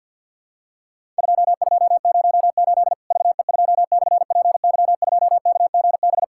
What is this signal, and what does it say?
Morse code, heard as a keyed tone, about 50 wpm, 700 Hz. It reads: 2208 VE2/PY2KGB
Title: Worked All Europe DX Contest - QTC Exchange - 50wpm